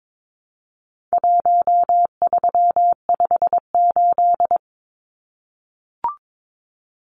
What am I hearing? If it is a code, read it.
Morse code, heard as a keyed tone, about 22 wpm, 700 Hz. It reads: 1358
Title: Numbers - Contest Style - 4-Digit 22wpm